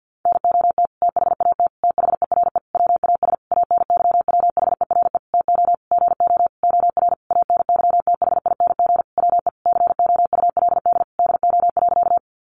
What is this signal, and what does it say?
Morse code, heard as a keyed tone, about 50 wpm, 700 Hz. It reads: NOT THAT THERE WAS ANYWHERE TO GO OR ANYTHING WE COULD DO.
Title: Sentences from Top 1400 Words 50wpm